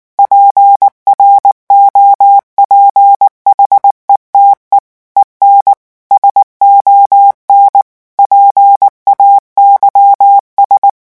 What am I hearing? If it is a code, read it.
Morse code, heard as a keyed tone, about 19 wpm, 800 Hz. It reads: PROPHETE EN SON PAYS
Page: REVISION 8 - 8 mots minute